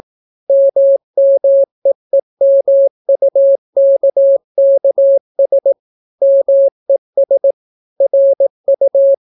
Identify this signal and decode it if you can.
Morse code, heard as a keyed tone, about 18 wpm, 550 Hz. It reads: MMEEMUKKS MES RU